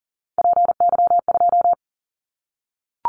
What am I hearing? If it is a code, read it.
Morse code, heard as a keyed tone, about 40 wpm, 690 Hz. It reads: PY2
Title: Callsign Prefixes for Top 100 DXCC - 40wpm